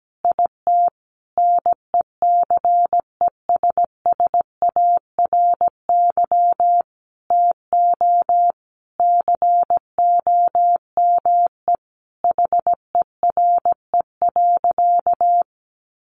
Morse code, heard as a keyed tone, about 17 wpm, 700 Hz. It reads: IT NECESSARY TO COME HERE.
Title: Sentences from Top 1300 Words 17wpm